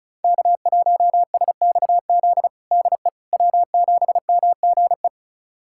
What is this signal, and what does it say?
Morse code, heard as a keyed tone, about 36 wpm, 700 Hz. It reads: K1SXZ DE W7MGE
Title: USA Callsign DE Callsign - 5 Characters - 35wpm